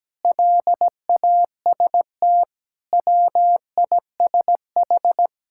Morse code, heard as a keyed tone, about 17 wpm, 700 Hz. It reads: LAST WISH
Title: Sentences from Top 1600 Words 17wpm